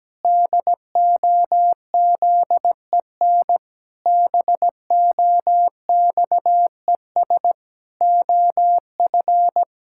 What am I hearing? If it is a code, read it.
Morse code, heard as a keyed tone, about 17 wpm, 700 Hz. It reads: DOZEN BOXES OF
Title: Pangram Challenge v2 - 17wpm